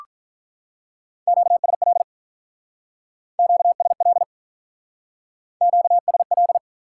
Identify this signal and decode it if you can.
Morse code, heard as a keyed tone, about 46 wpm, 700 Hz. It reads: QSL QSL QSL